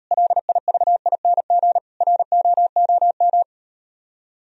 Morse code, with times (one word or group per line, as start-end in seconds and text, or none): 0.1-1.8 LIVING
2.0-3.4 ROOM